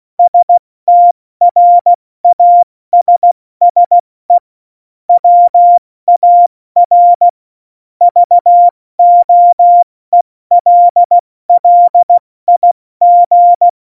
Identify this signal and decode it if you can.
Morse code, heard as a keyed tone, about 16 wpm, 700 Hz. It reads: STRASSE WAR VOELLIG
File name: Morsekurs / Witze / Witz 05. Januar 2021, 70 F80